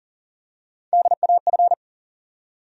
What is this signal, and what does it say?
Morse code, heard as a keyed tone, about 40 wpm, 700 Hz. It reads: DAF
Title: USA Call Sign Postfixes - 3 Letters 40wpm